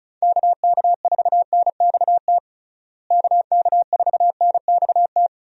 Morse code, heard as a keyed tone, about 34 wpm, 700 Hz. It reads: KK4NXT KK4NXT
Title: USA Call Signs - 6 Characters - Repeated 2x - 35wpm